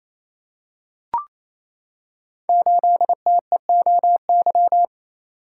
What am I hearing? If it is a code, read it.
Morse code, heard as a keyed tone, about 29 wpm, 700 Hz. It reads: 8TEOY
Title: Code Groups - 5 Letter:Numbers - 28wpm